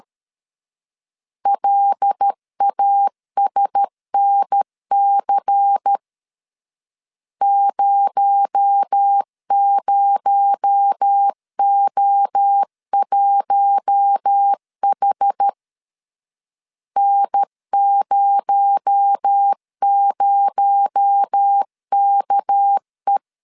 Morse code, heard as a keyed tone, about 13 wpm, 800 Hz. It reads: LASNC 00O1H N00KE
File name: LEZIONI / 6 PKX0 / 6 PKX0 12wpm 2231